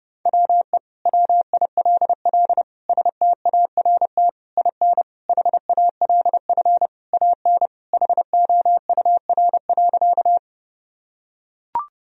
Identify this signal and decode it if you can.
Morse code, heard as a keyed tone, about 31 wpm, 700 Hz. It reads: WE WILL START IN HALF AN HOUR.
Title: Sentences from Top 1000 Words - Review Part 1 - 30wpm